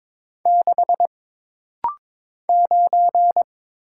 Morse code, heard as a keyed tone, about 22 wpm, 700 Hz. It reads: 6 9
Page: This rapid-fire timing is geared to push your ICR (Instant Character Recognition) skills.